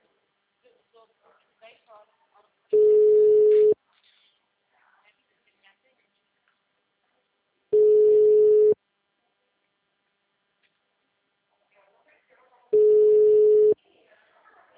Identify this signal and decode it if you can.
Ringback tone (Europe, 425 Hz), with speech.